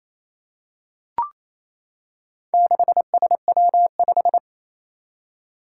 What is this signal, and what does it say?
Morse code, heard as a keyed tone, about 28 wpm, 700 Hz. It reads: 6SW5
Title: Four Challenging Letter-Number Combinations 28wpm